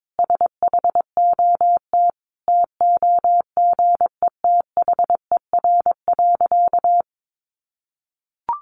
Morse code, heard as a keyed tone, about 22 wpm, 700 Hz. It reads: SHOT TOGETHER.
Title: Sentences from Top 1100 Words 22wpm